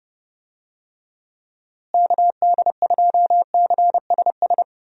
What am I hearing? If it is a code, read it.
Morse code, heard as a keyed tone, about 29 wpm, 700 Hz. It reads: KD2CSS
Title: USA Call Signs - 6 Characters - Repeated 2x - 30wpm